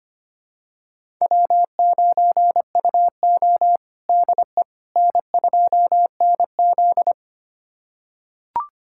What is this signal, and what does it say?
Morse code, heard as a keyed tone, about 25 wpm, 700 Hz. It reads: W9UO DE N2NZ
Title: USA Callsign DE Callsign - 4 Characters - 25wpm